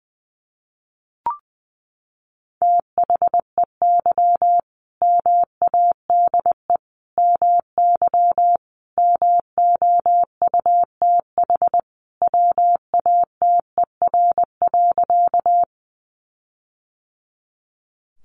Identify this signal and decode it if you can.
Morse code, heard as a keyed tone, about 20 wpm, 700 Hz. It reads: THEY MADE MY MOUTH WATER.
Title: Sentences from Top 1500 Words 20wpm